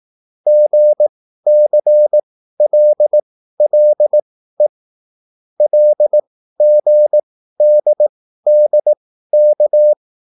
Morse code, heard as a keyed tone, about 18 wpm, spaced as for 13 wpm, 600 Hz. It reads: GCLLE LGDDK
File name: QSL / CW lesson